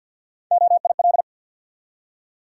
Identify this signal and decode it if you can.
Morse code, heard as a keyed tone, about 53 wpm, 700 Hz. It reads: OIL
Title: Three Letter Words - 50wpm